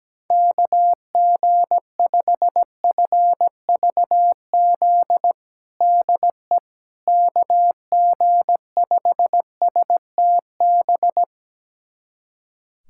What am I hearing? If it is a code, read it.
Morse code, heard as a keyed tone, about 17 wpm, 700 Hz. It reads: KG5FVZ DE KG5STB
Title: USA Callsign DE Callsign - 6 Characters - 17wpm